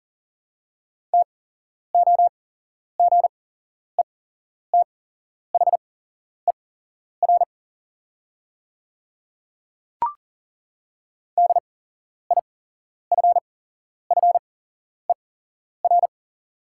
Morse code, heard as a keyed tone, about 41 wpm, spaced as for 11 wpm, 690 Hz. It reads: TOGETHER DIFFER
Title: 8x Character Spacing - Top 300 Words - 40wpm